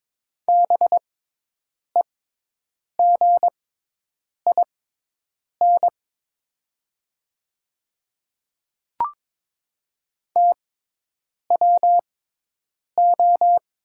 Morse code, heard as a keyed tone, about 22 wpm, spaced as for 8 wpm, 700 Hz. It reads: BEGIN TWO
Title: 6x Character Spacing - Top 500 Words - 22wpm